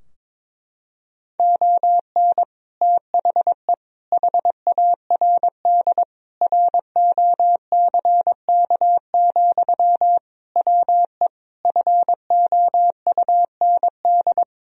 Morse code, heard as a keyed tone, about 22 wpm, 700 Hz. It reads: ON THE HARD ROCK, WE FOUND
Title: Sentences from Top 1000 Words - Review Part 2 - 22wpm